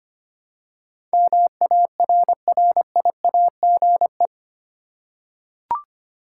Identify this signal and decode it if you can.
Morse code, heard as a keyed tone, about 25 wpm, 700 Hz. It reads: MARRIAGE
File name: 8 Letter Words - First 3 Letters Given - 25wpm